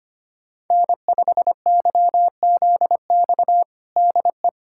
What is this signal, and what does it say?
Morse code, heard as a keyed tone, about 25 wpm, 700 Hz. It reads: N5YZX DE
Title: USA Callsign DE Callsign - 5 Characters - 25wpm